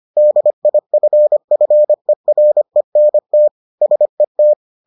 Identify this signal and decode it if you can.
Morse code, heard as a keyed tone, about 25 wpm, 600 Hz. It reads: DIFFERENT SET